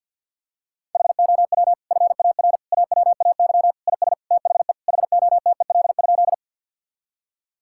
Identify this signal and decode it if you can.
Morse code, heard as a keyed tone, about 51 wpm, 700 Hz. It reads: HOW FAR AWAY IS THE HOTEL?
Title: Sentences from Top 1600 Words 50wpm